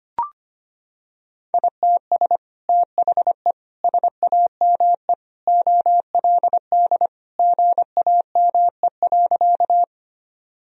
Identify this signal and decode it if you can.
Morse code, heard as a keyed tone, about 25 wpm, 700 Hz. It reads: ITS THE SAME OLD GAME.
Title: Sentences from Top 1000 Words - Review Part 3 - 25wpm